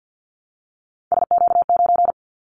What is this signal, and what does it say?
Morse code, heard as a keyed tone, about 50 wpm, 700 Hz. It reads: SQ9
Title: Callsign Prefixes for Top 25 DXCC - 50wpm